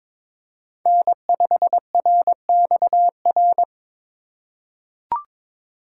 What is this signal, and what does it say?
Morse code, heard as a keyed tone, about 22 wpm, 700 Hz. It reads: N5RXR
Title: USA Call Signs - 5 Characters - 4x Character Spacing - 22wpm